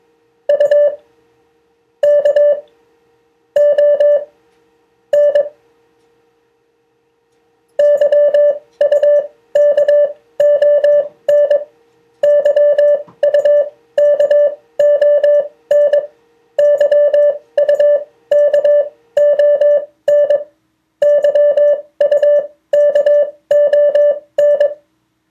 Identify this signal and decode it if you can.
Morse code, heard as a keyed tone, about 21 wpm, spaced as for 14 wpm, 580 Hz. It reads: UKON YUKON YUKON YUKON YUKON